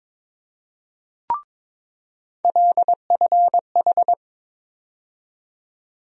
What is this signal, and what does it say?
Morse code, heard as a keyed tone, about 22 wpm, 700 Hz. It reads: LFH
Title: USA Call Sign Postfixes - 3 Letters 22wpm